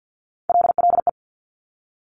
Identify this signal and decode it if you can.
Morse code, heard as a keyed tone, about 50 wpm, 700 Hz. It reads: LLE
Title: Top Three Letter Combinations in Words 50wpm